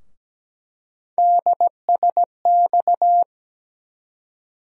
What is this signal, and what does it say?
Morse code, heard as a keyed tone, about 17 wpm, 700 Hz. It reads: DSX
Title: USA Call Sign Postfixes - 3 Letters 17wpm